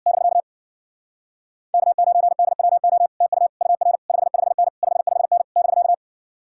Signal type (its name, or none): Morse code